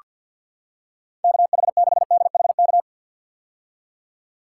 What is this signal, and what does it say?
Morse code, heard as a keyed tone, about 48 wpm, 700 Hz. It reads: KH6DHK
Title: K1USN SST QSO Practice - 50wpm